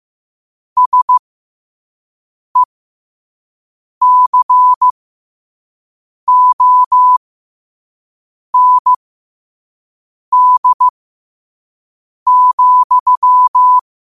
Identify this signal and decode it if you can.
Morse code, heard as a keyed tone, about 15 wpm, spaced as for 5 wpm, 1000 Hz. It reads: SECOND,
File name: qotdfiles / 5wpm